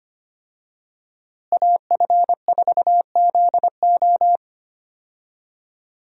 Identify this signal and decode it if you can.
Morse code, heard as a keyed tone, about 25 wpm, 700 Hz. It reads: AF4ZO
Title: USA Call Signs - 5 Characters - Repeated 3x - 25wpm